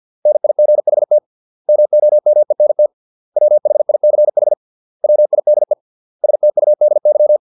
Morse code, heard as a keyed tone, about 50 wpm, 600 Hz. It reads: NIGHT MOMENT WHICH WIDE STUDY